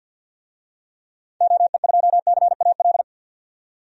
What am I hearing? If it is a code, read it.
Morse code, heard as a keyed tone, about 51 wpm, 700 Hz. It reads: OE2CAL
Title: DXCC-100 Call Signs - 6 Characters - 50wpm